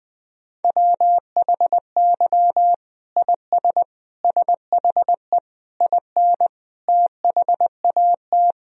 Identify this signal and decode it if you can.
Morse code, heard as a keyed tone, about 20 wpm, 700 Hz. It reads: WHY IS SHE IN THAT
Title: Sentences from Top 1000 Words - Review Part 2 - 20wpm